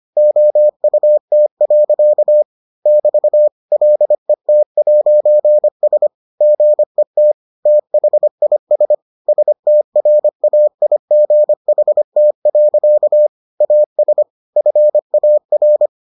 Morse code, heard as a keyed tone, about 25 wpm, 600 Hz. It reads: OUT. <BT> LET'S GET THIS STRAIGHT. AS FAR